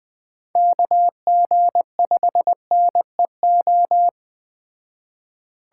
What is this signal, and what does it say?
Morse code, heard as a keyed tone, about 20 wpm, 700 Hz. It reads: KG5NEO